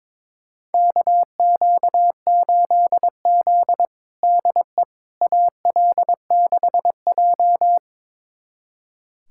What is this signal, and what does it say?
Morse code, heard as a keyed tone, about 22 wpm, 700 Hz. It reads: KQ8Z DE AL6J
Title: USA Callsign DE Callsign - 4 Characters - 22wpm